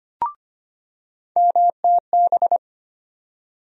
Morse code, heard as a keyed tone, about 25 wpm, 700 Hz. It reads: MTB
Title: USA Call Sign Postfixes - 3 Letters 25wpm